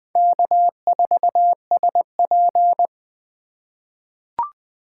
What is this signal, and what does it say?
Morse code, heard as a keyed tone, about 20 wpm, 700 Hz. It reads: K4SP
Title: 2023 POTA Hunter Call Signs - 20wpm